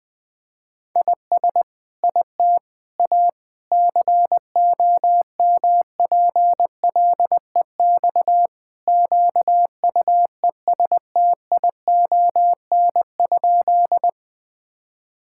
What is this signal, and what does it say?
Morse code, heard as a keyed tone, about 20 wpm, 700 Hz. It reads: IS IT A COMPLEX QUESTION?
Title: Sentences from Top 1000 Words - Review Part 6 - 20wpm